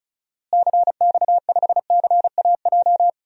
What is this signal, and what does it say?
Morse code, heard as a keyed tone, about 34 wpm, 700 Hz. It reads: CX5CAJ
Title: DXCC-100 Call Signs - 6 Characters - Repeated 2x - 35wpm